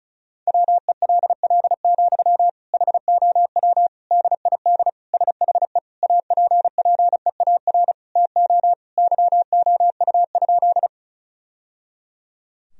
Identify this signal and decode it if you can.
Morse code, heard as a keyed tone, about 36 wpm, 700 Hz. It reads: WELL, HOW DID SHE APPEAR TO YOU?
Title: Sentences from Top 1000 Words - Review Part 2 - 35wpm